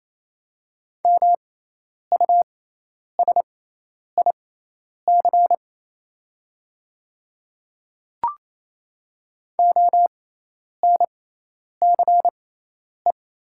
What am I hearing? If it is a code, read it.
Morse code, heard as a keyed tone, about 28 wpm, spaced as for 10 wpm, 700 Hz. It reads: MUSIC ONCE
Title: 6x Character Spacing - Top 300 Words - 28wpm